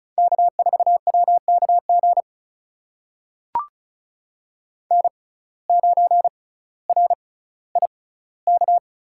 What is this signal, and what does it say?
Morse code, heard as a keyed tone, about 36 wpm, spaced as for 18 wpm, 700 Hz. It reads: K4WKG N9RIK
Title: USA Call Signs - 5 Characters - 6x Character Spacing - 35wpm